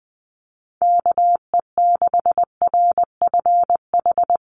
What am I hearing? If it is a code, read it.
Morse code, heard as a keyed tone, about 20 wpm, 700 Hz. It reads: KE6RFH
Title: USA Call Signs Repeated 3x 20wpm